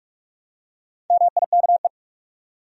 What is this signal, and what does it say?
Morse code, heard as a keyed tone, about 46 wpm, 700 Hz. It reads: MIKE
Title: USA Common Names - 45wpm